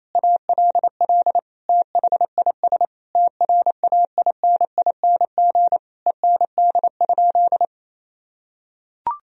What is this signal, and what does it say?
Morse code, heard as a keyed tone, about 28 wpm, 700 Hz. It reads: ALL THIS TRAINING END?
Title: Sentences from Top 1000 Words - Review Part 4 - 28wpm